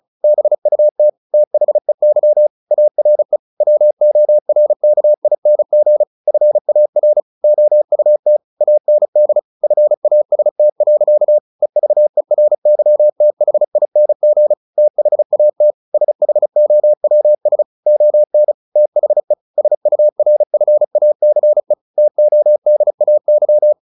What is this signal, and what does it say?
Morse code, heard as a keyed tone, about 34 wpm, 600 Hz. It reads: BUT THEY ARE WORKING FAR OUT AND FAST. EVERYTHING THAT SHOWS ON THE SURFACE TODAY
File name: the old man and the sea / the old man and the sea 1 35wpm